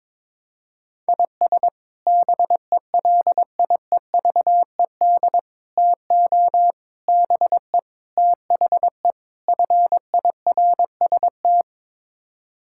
Morse code, heard as a keyed tone, about 22 wpm, 700 Hz. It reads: IS BELIEVED TO BE THE FIRST
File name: practice / 22wpm / Sets of 6 Words - 22wpm